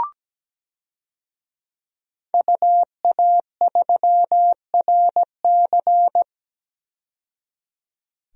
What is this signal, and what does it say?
Morse code, heard as a keyed tone, about 17 wpm, 700 Hz. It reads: UA3RC
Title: DXCC-100 Call Signs - 5 Characters - 17wpm